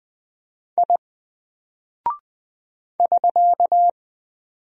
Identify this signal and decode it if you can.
Morse code, heard as a keyed tone, about 20 wpm, 700 Hz. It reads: I <SK>
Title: Single Character (Letter:Number:Punctuation:Prosign) 20wpm